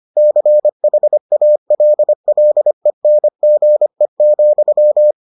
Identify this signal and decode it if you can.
Morse code, heard as a keyed tone, about 25 wpm, 600 Hz. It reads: CHALLENGE,